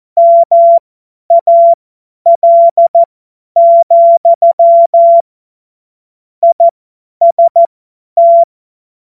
Morse code, heard as a keyed tone, about 14 wpm, spaced as for 10 wpm, 680 Hz. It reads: MAL, IST
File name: Morsekurs / Witze / Witz 25. April 2023 50 F70